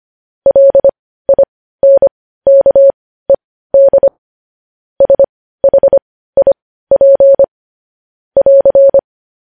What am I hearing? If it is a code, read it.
Morse code, heard as a keyed tone, about 25 wpm, spaced as for 15 wpm, 550 Hz. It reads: LINKED SHIP <AR>